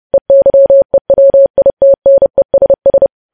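Morse code, heard as a keyed tone, about 30 wpm, 550 Hz. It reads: EYEWITNESS
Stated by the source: Morse code practice of news headlines at 30 wpm, preceded by vvv.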